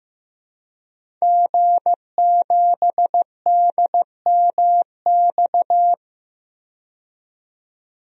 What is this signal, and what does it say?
Morse code, heard as a keyed tone, about 16 wpm, 700 Hz. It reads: G7DMX
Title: DXCC-100 Call Signs - 5 Characters - 15wpm